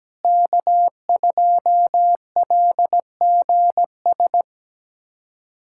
Morse code, heard as a keyed tone, about 17 wpm, 700 Hz. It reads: K2LGS
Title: USA Callsign DE Callsign - 5 Characters - 17wpm